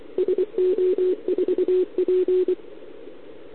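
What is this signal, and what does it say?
Morse code, heard as a keyed tone, about 23 wpm, 370 Hz. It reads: SO4P
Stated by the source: Contest CW from OK1RI on 80 m during the OKOM DX contest, recorded by a QRP station using a KX3 and a wire antenna.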